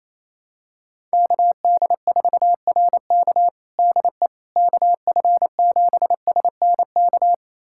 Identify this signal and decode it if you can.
Morse code, heard as a keyed tone, about 28 wpm, 700 Hz. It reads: KD4RK DE KF7SNK
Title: USA Callsign DE Callsign 28wpm